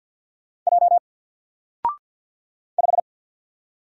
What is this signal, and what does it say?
Morse code, heard as a keyed tone, about 50 wpm, 700 Hz. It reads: J 5
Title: Single Letter-Number Alt 50wpm